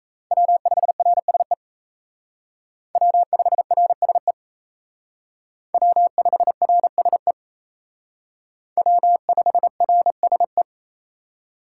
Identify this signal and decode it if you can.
Morse code, heard as a keyed tone, about 36 wpm, 700 Hz. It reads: W5RSE W5RSE W5RSE W5RSE